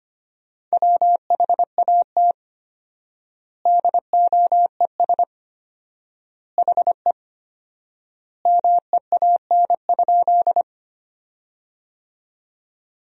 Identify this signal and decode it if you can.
Morse code, heard as a keyed tone, about 25 wpm, 700 Hz. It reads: WHAT DOES HE MEAN?
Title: 4x Word-Spacing - Sentences from Top 200 Words - 25wpm